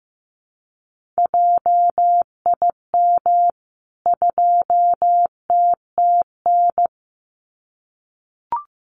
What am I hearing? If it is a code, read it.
Morse code, heard as a keyed tone, about 15 wpm, 700 Hz. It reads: JIM 2TTN
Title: CWT Exchange 15wpm